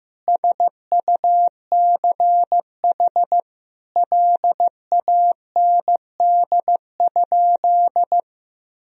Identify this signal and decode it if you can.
Morse code, heard as a keyed tone, about 15 wpm, 700 Hz. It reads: SUCH LAND?
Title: Sentences from Top 1000 Words - Review Part 1 - 15wpm